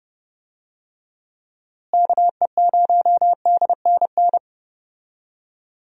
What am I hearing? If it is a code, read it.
Morse code, heard as a keyed tone, about 32 wpm, 700 Hz. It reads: KE0DNN
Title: USA Call Signs - 6 Characters - Repeated 2x - 30wpm